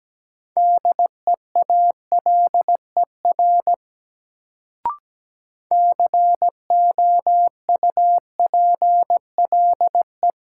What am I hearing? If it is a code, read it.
Morse code, heard as a keyed tone, about 17 wpm, 700 Hz. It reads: DEALER COUPLE
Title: Sentences from Top 1100 Words - New Word Drill - 17wpm